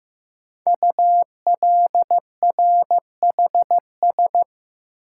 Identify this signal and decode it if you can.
Morse code, heard as a keyed tone, about 15 wpm, 700 Hz. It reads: ULRHS
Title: Code Groups - 5 Letters - 15wpm